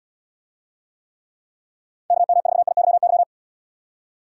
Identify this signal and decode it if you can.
Morse code, heard as a keyed tone, about 77 wpm, 700 Hz. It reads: KM4IYY